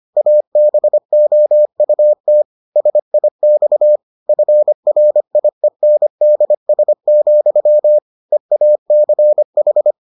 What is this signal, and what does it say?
Morse code, heard as a keyed tone, about 25 wpm, 600 Hz. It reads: ABOUT SIX FRIENDS, EACH